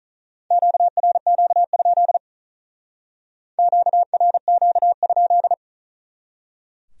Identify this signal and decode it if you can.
Morse code, heard as a keyed tone, about 41 wpm, 700 Hz. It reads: QRQ? QRQ?
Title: Speed-Racing Q-Codes - 35wpm